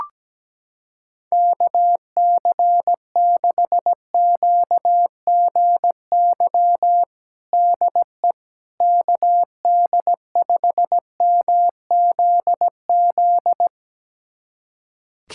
Morse code, heard as a keyed tone, about 17 wpm, 700 Hz. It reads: KC6QGY DE KD5MZZ
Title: USA Callsign DE Callsign - 6 Characters - 17wpm